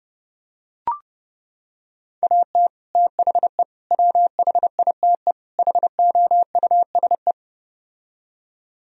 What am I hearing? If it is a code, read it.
Morse code, heard as a keyed tone, about 32 wpm, 700 Hz. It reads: AT THE WHITE HOUSE